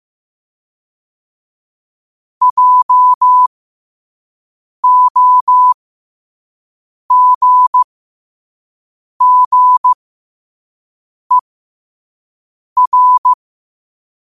Morse code, heard as a keyed tone, about 15 wpm, spaced as for 5 wpm, 1000 Hz. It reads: JOGGER
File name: qotdfiles / 5wpm